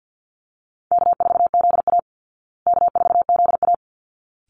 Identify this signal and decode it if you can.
Morse code, heard as a keyed tone, about 50 wpm, 700 Hz. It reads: K4ZA K4ZA
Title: Easy Call Signs Repeated 2x - 50wpm